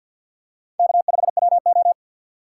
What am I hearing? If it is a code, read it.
Morse code, heard as a keyed tone, about 53 wpm, 700 Hz. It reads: K5WO
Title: USA Call Signs - 4 Characters - 6x Character Spacing - 50wpm